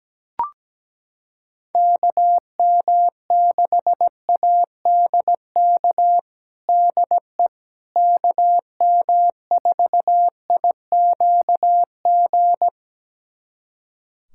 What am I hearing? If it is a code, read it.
Morse code, heard as a keyed tone, about 17 wpm, 700 Hz. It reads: KM6ADK DE KM4IQG
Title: USA Callsign DE Callsign - 6 Characters - 17wpm